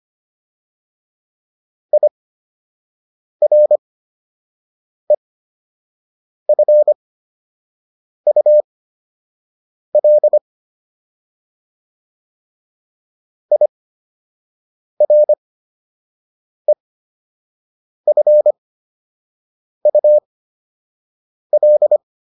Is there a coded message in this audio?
Yes: Morse code from 1.9 to 22.0 s, heard as a keyed tone, about 25 wpm, spaced as for 6 wpm, 600 Hz. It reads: IREFUL IREFUL